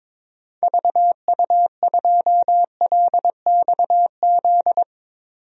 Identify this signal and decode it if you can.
Morse code, heard as a keyed tone, about 22 wpm, 700 Hz. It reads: VU2LXZ
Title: DXCC-100 Call Signs - 6 Characters - 22wpm